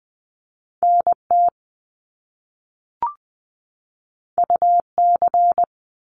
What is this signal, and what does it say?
Morse code, heard as a keyed tone, about 20 wpm, 700 Hz. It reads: NT UC
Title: Top Two Letter Combinations in Words 20wpm